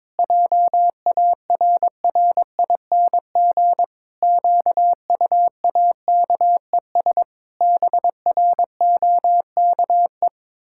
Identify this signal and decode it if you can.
Morse code, heard as a keyed tone, about 22 wpm, 700 Hz. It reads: JARRING QUAKES BROKE
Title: Pangram Challenge v2 - 22wpm